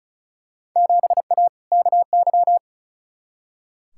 Morse code, heard as a keyed tone, about 34 wpm, 700 Hz. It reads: 7A KY
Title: Field Day Exchanges Encore - 35wpm